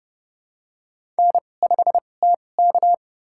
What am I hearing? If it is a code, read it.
Morse code, heard as a keyed tone, about 30 wpm, spaced as for 22 wpm, 700 Hz. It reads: N5TK
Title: USA Call Signs - 4 Characters - 2x Character Spacing - 30wpm